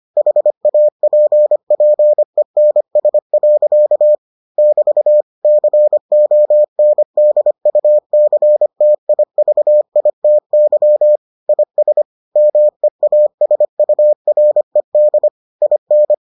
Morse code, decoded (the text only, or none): HAPPENS. <BT> CONDUCTIVITY IS MEASURED IN